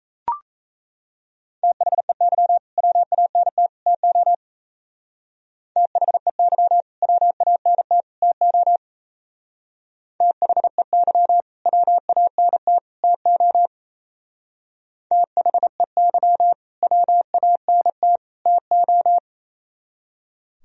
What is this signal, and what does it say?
Morse code, heard as a keyed tone, about 35 wpm, 700 Hz. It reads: THEY WANT TO THEY WANT TO THEY WANT TO THEY WANT TO
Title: Speed-Racing - Sets of 3 Words 28wpm